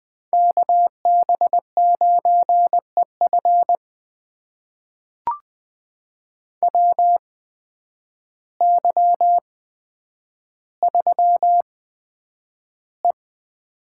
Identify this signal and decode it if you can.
Morse code, heard as a keyed tone, about 20 wpm, spaced as for 9 wpm, 700 Hz. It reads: KB9EF WY3E
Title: USA Call Signs - 8x Character Spacing - 20wpm